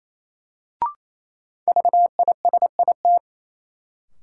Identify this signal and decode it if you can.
Morse code, heard as a keyed tone, about 28 wpm, 700 Hz. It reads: VISIT
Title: Sentences from Top 900 Words - New Word Drill - 28wpm